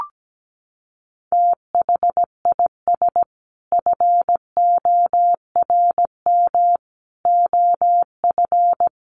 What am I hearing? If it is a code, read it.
Morse code, heard as a keyed tone, about 17 wpm, 700 Hz. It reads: THIS FORM OF
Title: Sentences from Top 1100 Words 17wpm